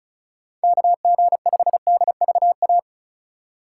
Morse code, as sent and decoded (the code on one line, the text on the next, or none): -.- --. ..... -.. ...- .-
KG5DVA